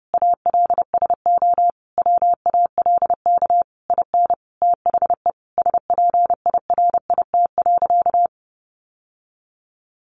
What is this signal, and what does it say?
Morse code, heard as a keyed tone, about 30 wpm, 700 Hz. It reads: ALSO WALK IN THE SPIRIT.
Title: Sentences from Top 1400 Words 30wpm